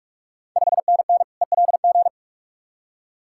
Morse code, heard as a keyed tone, about 46 wpm, 700 Hz. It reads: 5NN ELG
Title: Ontario QSO Party Exchange 45wpm